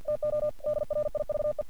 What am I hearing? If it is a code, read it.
Morse code, heard as a keyed tone, about 48 wpm, 600 Hz. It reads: TO DRIVE
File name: archived / CRSnet